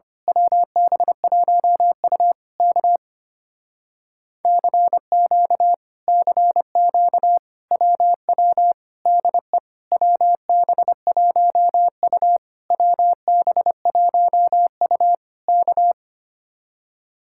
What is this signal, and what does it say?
Morse code, heard as a keyed tone, about 24 wpm, 700 Hz. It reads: WB1U K CQ CQ WW DE WB1U WB1U K
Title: Speed-Racing Common QSO Phrases - 25wpm